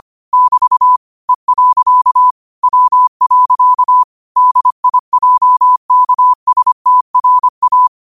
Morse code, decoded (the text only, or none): <BT> E. W. DIJKSTRA